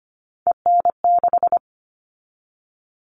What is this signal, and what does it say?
Morse code, heard as a keyed tone, about 25 wpm, 700 Hz. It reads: EN6
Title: All Cut Numbers - 3-Digit 25wpm